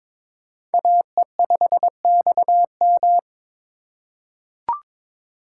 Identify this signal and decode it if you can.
Morse code, heard as a keyed tone, about 22 wpm, 700 Hz. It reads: AE5XM
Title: USA Call Signs - 5 Characters - 4x Character Spacing - 22wpm